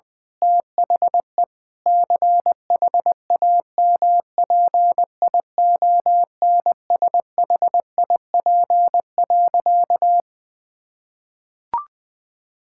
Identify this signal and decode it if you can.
Morse code, heard as a keyed tone, about 21 wpm, 700 Hz. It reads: THE CHAMPIONSHIP.
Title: Sentences with 8-Word Phrases - 20wpm